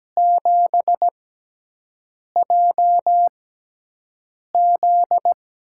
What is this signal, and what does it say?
Morse code, heard as a keyed tone, about 17 wpm, 700 Hz. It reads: 7 J Z
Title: USA Call Signs - 5 Characters - 6x Character Spacing - 17wpm